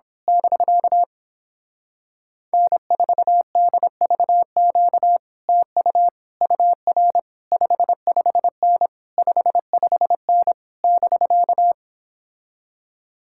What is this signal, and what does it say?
Morse code, heard as a keyed tone, about 26 wpm, 700 Hz. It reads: <BK> N4DVQ TU UR 55N 55N <BK>
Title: Speed-Racing Common QSO Phrases - 22wpm